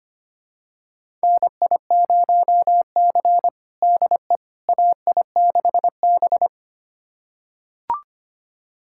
Morse code, heard as a keyed tone, about 25 wpm, 700 Hz. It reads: NI0C DE AI6B
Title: USA Callsign DE Callsign - 4 Characters - 25wpm